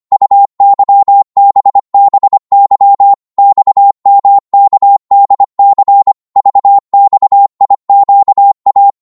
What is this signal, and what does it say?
Morse code, heard as a keyed tone, about 25 wpm, 800 Hz. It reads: UYBBY XMKDC VXIQA
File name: Morse / ltr-25wpm